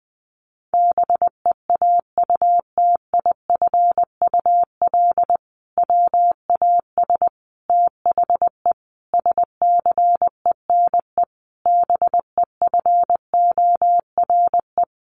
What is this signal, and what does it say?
Morse code, heard as a keyed tone, about 20 wpm, 700 Hz. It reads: BEAUTIFUL WAS THE SCENE BEFORE
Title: Sentences from Top 1100 Words 20wpm